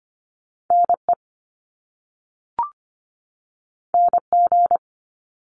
Morse code, heard as a keyed tone, about 25 wpm, 700 Hz. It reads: NE NG
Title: Top Two Letter Combinations in Words 25wpm